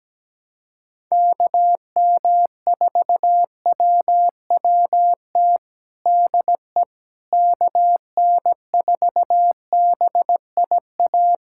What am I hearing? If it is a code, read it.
Morse code, heard as a keyed tone, about 17 wpm, 700 Hz. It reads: KM4WWT DE KN4BIA
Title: USA Callsign DE Callsign 17wpm